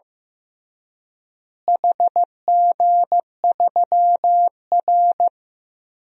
Morse code, heard as a keyed tone, about 15 wpm, 700 Hz. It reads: HG3R
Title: DXCC-100 Call Signs - 4 Characters - Repeated 2x - 15wpm